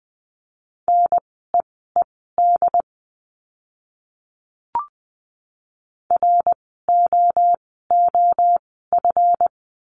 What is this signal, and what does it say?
Morse code, heard as a keyed tone, about 20 wpm, spaced as for 15 wpm, 700 Hz. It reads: NEED ROOF
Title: 139 Lesson 23f - Letter f - Words 20wpm